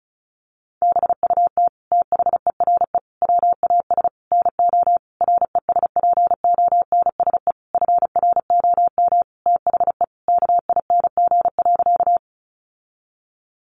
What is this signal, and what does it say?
Morse code, heard as a keyed tone, about 34 wpm, 700 Hz. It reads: BUT THERE WAS NO RESPONSE FROM THE KING.
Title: Sentences from Top 1400 Words 35wpm